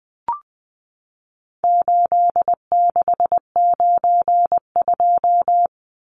Morse code, heard as a keyed tone, about 20 wpm, 700 Hz. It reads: 8692
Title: Numbers - 4-Digit 20wpm